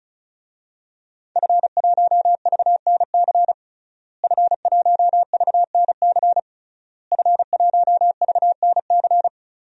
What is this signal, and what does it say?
Morse code, heard as a keyed tone, about 36 wpm, 700 Hz. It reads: F1VNC F1VNC F1VNC
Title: DXCC-100 Call Signs - 5 Characters - Repeated 3x - 35wpm